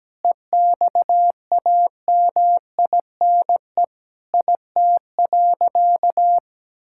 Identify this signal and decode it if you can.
Morse code, heard as a keyed tone, about 17 wpm, 700 Hz. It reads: EXAMINE IT.
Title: Sentences from Top 1600 Words 17wpm